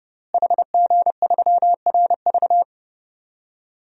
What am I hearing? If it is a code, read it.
Morse code, heard as a keyed tone, about 32 wpm, 700 Hz. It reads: HG3RV
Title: DXCC-100 Call Signs - 5 Characters - Repeated 2x - 30wpm